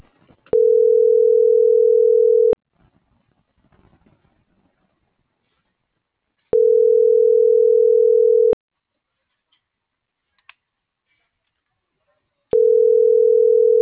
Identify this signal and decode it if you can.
Ringback tone (North America), 440 + 480 Hz.